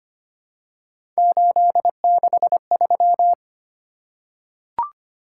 Morse code, heard as a keyed tone, about 25 wpm, 700 Hz. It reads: 863
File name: Three Letter-Number Combinations - 25wpm